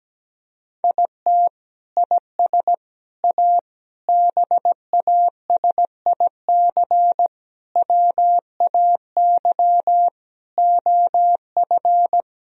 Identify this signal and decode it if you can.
Morse code, heard as a keyed tone, about 17 wpm, 700 Hz. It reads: IT IS A BASIC WAY OF
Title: Sentences from Top 1000 Words - Review Part 3 - 17wpm